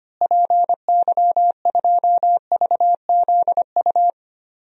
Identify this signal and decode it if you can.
Morse code, heard as a keyed tone, about 25 wpm, 700 Hz. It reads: PY2VZU
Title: DXCC-100 Call Signs - 6 Characters - Repeated 3x - 25wpm